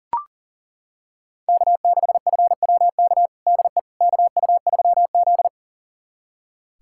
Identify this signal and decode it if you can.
Morse code, heard as a keyed tone, about 37 wpm, 700 Hz. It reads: K6FWK DE KU3Z
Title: USA Callsign DE Callsign 40wpm